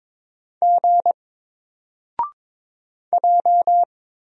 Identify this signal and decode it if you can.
Morse code, heard as a keyed tone, about 22 wpm, 700 Hz. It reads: G J
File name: Single-Letter-Number-22wpm